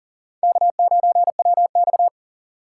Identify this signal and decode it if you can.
Morse code, heard as a keyed tone, about 40 wpm, 700 Hz. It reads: K9WX
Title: USA Call Signs - 4 Characters - 4x Character Spacing - 40wpm